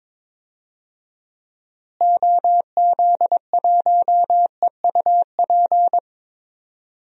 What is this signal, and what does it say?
Morse code, heard as a keyed tone, about 21 wpm, 700 Hz. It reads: OZ1EUP
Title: DXCC-100 Call Signs - 6 Characters - Repeated 2x - 22wpm